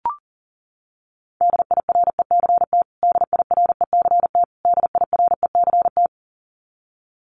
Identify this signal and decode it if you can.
Morse code, heard as a keyed tone, about 40 wpm, 700 Hz. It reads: DIRECT DIRECT DIRECT
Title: Top 500 Words Repeated 40wpm